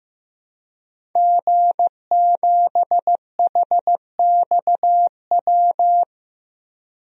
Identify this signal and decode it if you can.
Morse code, heard as a keyed tone, about 15 wpm, 700 Hz. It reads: G7HXW
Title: DXCC-100 Call Signs - 5 Characters - Repeated 2x - 15wpm